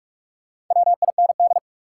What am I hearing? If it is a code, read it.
Morse code, heard as a keyed tone, about 45 wpm, 700 Hz. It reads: WIND
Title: Sentences from Top 400 Words - New Word Drill - 45wpm